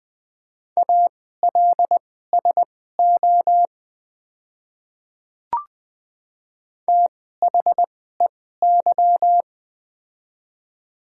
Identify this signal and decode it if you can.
Morse code, heard as a keyed tone, about 20 wpm, spaced as for 14 wpm, 700 Hz. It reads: ALSO THEY
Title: 2x Character Spacing - Top 300 Words - 20wpm